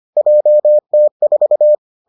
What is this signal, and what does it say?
Morse code, heard as a keyed tone, about 25 wpm, 600 Hz. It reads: JT4